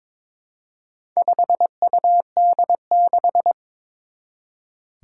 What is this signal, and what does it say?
Morse code, heard as a keyed tone, about 22 wpm, 700 Hz. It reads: 5UD6
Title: Four Challenging Letter-Number Combinations 22wpm